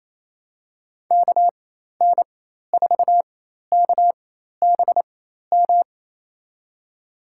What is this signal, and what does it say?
Morse code, heard as a keyed tone, about 28 wpm, spaced as for 13 wpm, 700 Hz. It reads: KN4KBM
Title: USA Call Signs - 6 Characters - 4x Character Spacing - 28wpm